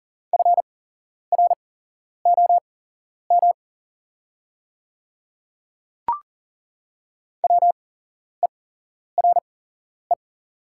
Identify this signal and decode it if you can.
Morse code, heard as a keyed tone, about 40 wpm, spaced as for 11 wpm, 690 Hz. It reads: FROM WERE
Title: 8x Character Spacing - Top 100 Words - 40wpm